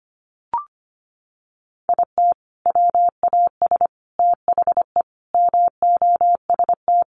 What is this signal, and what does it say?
Morse code, heard as a keyed tone, about 25 wpm, 700 Hz. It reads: IT WAS THE MOST
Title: Sentences from Top 1400 Words 25wpm